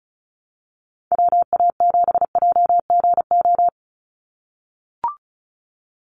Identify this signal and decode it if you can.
Morse code, heard as a keyed tone, about 34 wpm, 700 Hz. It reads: WA7JGO
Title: USA Call Signs 35wpm